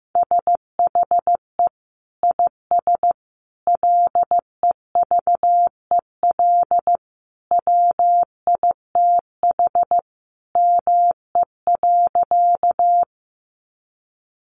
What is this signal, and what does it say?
Morse code, heard as a keyed tone, about 15 wpm, 700 Hz. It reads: SHE IS LEVEL WITH ME.
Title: Sentences from Top 700 Words 15wpm